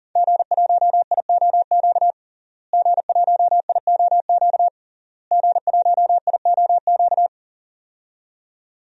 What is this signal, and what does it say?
Morse code, heard as a keyed tone, about 40 wpm, 700 Hz. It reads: G1IOQ G1IOQ G1IOQ
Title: DXCC-100 Call Signs - 5 Characters - Repeated 3x - 40wpm